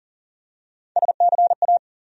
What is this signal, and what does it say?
Morse code, heard as a keyed tone, about 40 wpm, 700 Hz. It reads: SCA
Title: USA Call Sign Postfixes - 3 Letters 40wpm